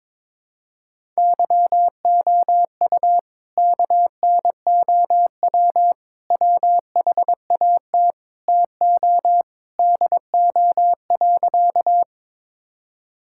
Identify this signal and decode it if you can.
Morse code, heard as a keyed tone, about 22 wpm, 700 Hz. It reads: YOU KNOW WHAT TO DO.
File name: practice / 22wpm / Sentences with 3-Word Phrases - 22wpm